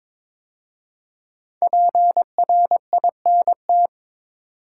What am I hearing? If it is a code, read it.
Morse code, heard as a keyed tone, about 22 wpm, 700 Hz. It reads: PRINT
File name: 5 Letter Words - First 3 Letters Given - 22wpm